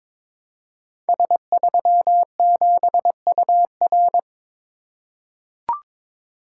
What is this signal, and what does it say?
Morse code, heard as a keyed tone, about 22 wpm, 700 Hz. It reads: S37UR
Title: Code Groups - 5 Letter:Numbers - 22wpm